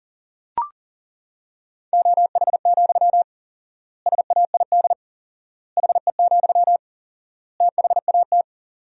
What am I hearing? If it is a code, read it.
Morse code, heard as a keyed tone, about 40 wpm, 700 Hz. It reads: OH, SAID HE, THAT
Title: 4x Word-Spacing - Sentences from Top 500 Words - 40wpm